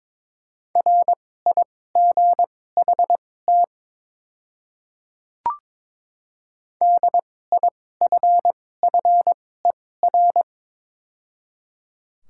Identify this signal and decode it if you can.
Morse code, heard as a keyed tone, about 21 wpm, spaced as for 16 wpm, 700 Hz. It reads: RIGHT DIFFER
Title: 2x Character Spacing - Top 200 Words - 22wpm